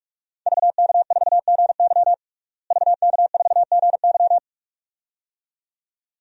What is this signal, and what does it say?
Morse code, heard as a keyed tone, about 46 wpm, 700 Hz. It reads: VK4GY VK4GY
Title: DXCC-100 Call Signs - 5 Characters - Repeated 2x - 45wpm